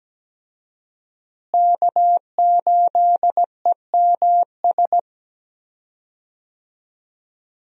Morse code, heard as a keyed tone, about 17 wpm, 700 Hz. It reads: K8EMS
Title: USA Call Signs - 5 Characters - Repeated 3x - 17wpm